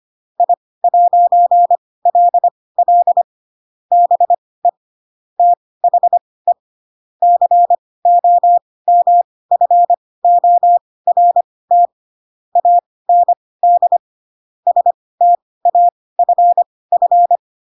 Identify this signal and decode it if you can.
Morse code, heard as a keyed tone, about 25 wpm, spaced as for 18 wpm, 700 Hz. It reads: I'LL BE THE COMFORT AND STAFF